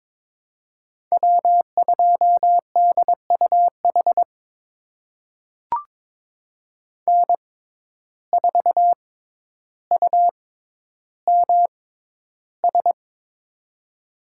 Morse code, heard as a keyed tone, about 22 wpm, spaced as for 11 wpm, 700 Hz. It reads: W2DUH N4UMS
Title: USA Call Signs - 5 Characters - 6x Character Spacing - 22wpm